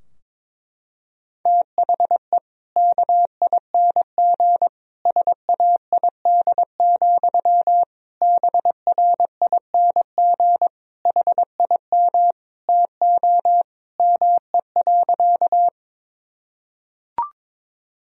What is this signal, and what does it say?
Morse code, heard as a keyed tone, about 22 wpm, 700 Hz. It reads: THE KING SAID, BRING HIM TO ME.
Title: Sentences from Top 1000 Words - Review Part 3 - 22wpm